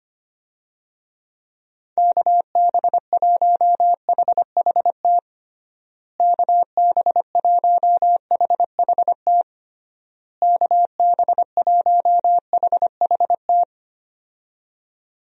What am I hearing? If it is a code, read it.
Morse code, heard as a keyed tone, about 25 wpm, 700 Hz. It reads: KB1HHT KB1HHT KB1HHT
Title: USA Call Signs - 6 Characters - Repeated 3x - 25wpm